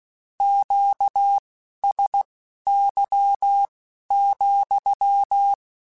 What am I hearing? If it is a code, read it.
Morse code, heard as a keyed tone, about 16 wpm, spaced as for 12 wpm, 800 Hz. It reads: QSY,
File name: Q Calls / Q Calls-mp3s / Q Calls sppc @15